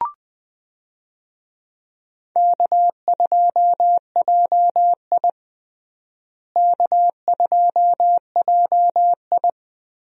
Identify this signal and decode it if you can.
Morse code, heard as a keyed tone, about 20 wpm, 700 Hz. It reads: K2JI K2JI
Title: USA Call Signs - 4 Characters - Repeated 3x - 20wpm